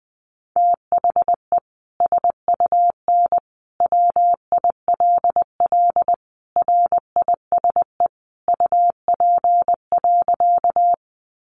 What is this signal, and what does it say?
Morse code, heard as a keyed tone, about 20 wpm, 700 Hz. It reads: THE SUN WILL RISE UP.
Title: Sentences from Top 900 Words 20wpm